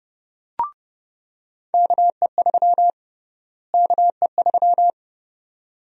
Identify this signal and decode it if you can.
Morse code, heard as a keyed tone, about 32 wpm, 700 Hz. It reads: KE3 KE3
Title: USA Call Sign Prefixes - Repeated 2x - 30wpm